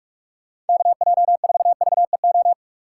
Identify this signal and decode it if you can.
Morse code, heard as a keyed tone, about 46 wpm, 700 Hz. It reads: KJ4VEO
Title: USA Call Signs - 6 Characters - 6x Character Spacing - 45wpm